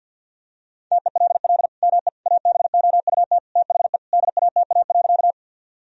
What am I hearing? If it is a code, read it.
Morse code, heard as a keyed tone, about 50 wpm, 700 Hz. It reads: TELL ME ABOUT THE DATA.
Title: Sentences from Top 1000 Words - Review Part 3 - 50wpm